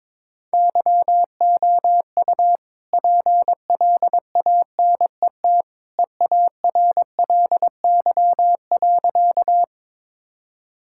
Morse code, heard as a keyed tone, about 22 wpm, 690 Hz. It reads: YOU PLANET EARLY.
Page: morse-code-ninja-practice-22wpm